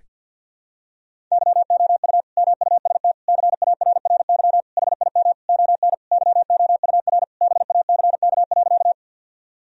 Morse code, heard as a keyed tone, about 51 wpm, 700 Hz. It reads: YOU MUST CARRY HIM ON YOUR BACK.
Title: Sentences from Top 1000 Words - Review Part 1 - 50wpm